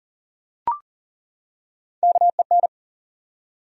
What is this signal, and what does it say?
Morse code, heard as a keyed tone, about 40 wpm, 700 Hz. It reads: KEN
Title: USA Common Names - Max 3 Letters - 40wpm